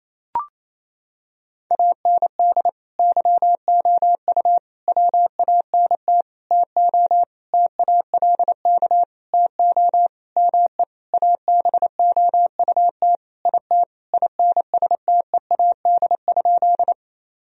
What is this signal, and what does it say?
Morse code, heard as a keyed tone, about 28 wpm, 700 Hz. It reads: AND YOU WANT TO TALK TO ME ABOUT IT INSTEAD?
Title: Sentences from Top 1000 Words - Review Part 4 - 28wpm